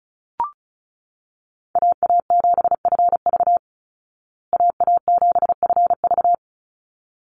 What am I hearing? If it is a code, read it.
Morse code, heard as a keyed tone, about 34 wpm, 700 Hz. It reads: AA7FV AA7FV
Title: Call Signs Repeated 3x - 35wpm